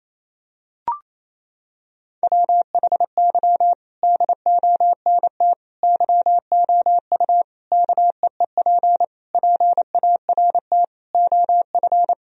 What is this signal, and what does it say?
Morse code, heard as a keyed tone, about 28 wpm, 700 Hz. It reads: WHY DONT YOU KEEP PART OF
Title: Sentences from Top 1000 Words - Review Part 4 - 28wpm